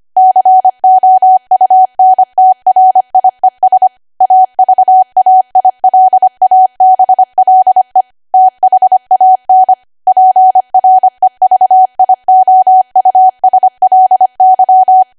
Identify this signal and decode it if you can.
Morse code, heard as a keyed tone, about 25 wpm, 750 Hz. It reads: COUNTRIES AVAILABLE THAN PREVIOUSLY